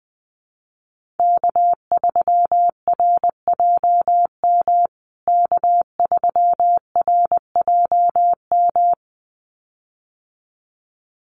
Morse code, heard as a keyed tone, about 19 wpm, 700 Hz. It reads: K3RJM K3RJM
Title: USA Call Signs Repeated 2x 20wpm